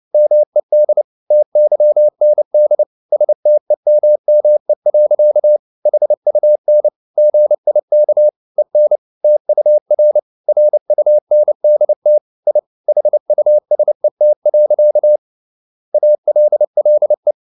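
Morse code, heard as a keyed tone, about 29 wpm, 600 Hz. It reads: MED TYND STEMME. HUN GIK EN TUR RUNDT I HUSET. ALLE
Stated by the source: Fast (QRQ) Morse at 29 wpm, from CW med Gnister.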